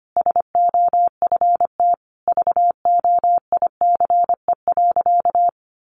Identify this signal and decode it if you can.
Morse code, heard as a keyed tone, about 25 wpm, 700 Hz. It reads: SOFT VOICE.
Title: Sentences from Top 1500 Words 25wpm